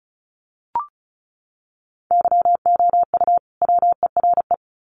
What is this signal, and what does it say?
Morse code, heard as a keyed tone, about 34 wpm, 700 Hz. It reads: YOU WERE
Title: Sets of 2 Words 35wpm